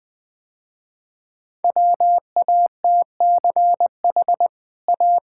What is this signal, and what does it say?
Morse code, heard as a keyed tone, about 20 wpm, 700 Hz. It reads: WATCH A
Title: Easy Sentences with Hints - 20wpm